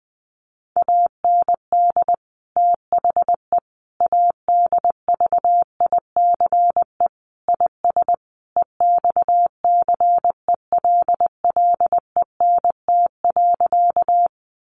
Morse code, heard as a keyed tone, about 20 wpm, 700 Hz. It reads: AND THE ADVICE IS EXCELLENT.
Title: Sentences from Top 1100 Words 20wpm